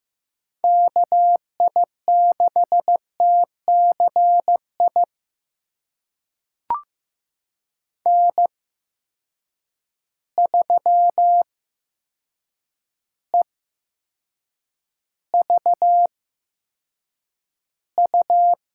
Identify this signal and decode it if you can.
Morse code, heard as a keyed tone, about 15 wpm, spaced as for 7 wpm, 700 Hz. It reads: KI6TCI N3EVU
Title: USA Call Signs - 8x Character Spacing - 15wpm